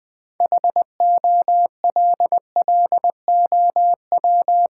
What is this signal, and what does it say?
Morse code, heard as a keyed tone, about 20 wpm, 690 Hz. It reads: HOLLOW
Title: 6 Letter Words - First 3 Letters Given - 20wpm